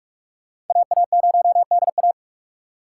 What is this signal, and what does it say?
Morse code, heard as a keyed tone, about 43 wpm, 690 Hz. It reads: AA0DA
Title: USA Call Signs - 4x Character Spacing - 45wpm